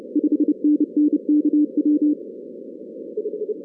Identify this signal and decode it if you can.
Morse code, heard as a keyed tone, about 28 wpm, 300 Hz. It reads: 5NN KW
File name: 2016 arrl dx cw / 2016-ARRL-DX-CW-K5TR- - 0-17